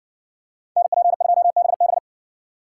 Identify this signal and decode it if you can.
Morse code, heard as a keyed tone, about 60 wpm, 700 Hz. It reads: NP2BB